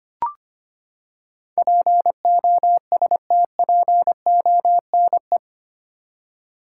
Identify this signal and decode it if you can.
Morse code, heard as a keyed tone, about 25 wpm, 700 Hz. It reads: POSTPONE
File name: Eight Letter Words - 25wpm